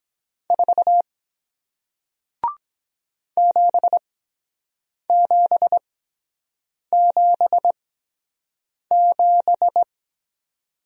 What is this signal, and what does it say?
Morse code, heard as a keyed tone, about 23 wpm, spaced as for 7 wpm, 700 Hz. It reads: 4 7777